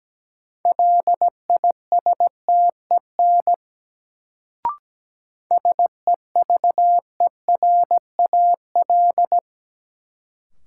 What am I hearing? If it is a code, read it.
Morse code, heard as a keyed tone, about 17 wpm, 700 Hz. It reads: LISTEN SEVERAL
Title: Sentences from Top 400 Words - New Word Drill - 17wpm